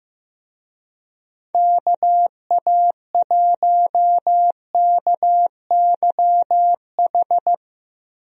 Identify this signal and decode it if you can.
Morse code, heard as a keyed tone, about 15 wpm, 700 Hz. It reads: KA1KYH
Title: USA Call Signs - 6 Characters - 15wpm